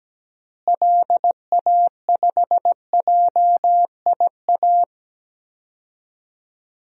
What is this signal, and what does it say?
Morse code, heard as a keyed tone, about 17 wpm, 700 Hz. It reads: LA5JIA
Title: DXCC-100 Call Signs - 6 Characters - Repeated 2x - 17wpm